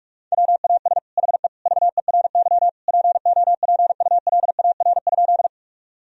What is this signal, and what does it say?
Morse code, heard as a keyed tone, about 45 wpm, 700 Hz. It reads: WAS HE VERY POPULAR?
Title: Sentences from Top 1000 Words - Review Part 3 - 45wpm